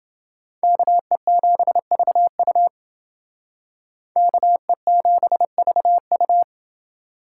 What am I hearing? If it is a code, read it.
Morse code, heard as a keyed tone, about 28 wpm, 700 Hz. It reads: KE7VU KE7VU